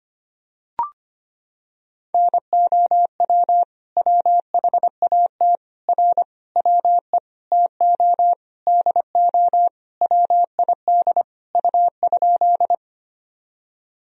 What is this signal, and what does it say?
Morse code, heard as a keyed tone, about 25 wpm, 700 Hz. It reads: NOW WHAT R WE TO DO WID U?
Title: Sentences from Top 100 Words and Abbreviations 25wpm